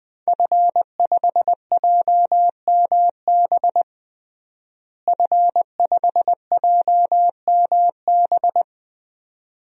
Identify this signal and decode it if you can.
Morse code, heard as a keyed tone, about 20 wpm, 700 Hz. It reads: F5JMB F5JMB
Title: DXCC-100 Call Signs - 5 Characters - Repeated 2x - 20wpm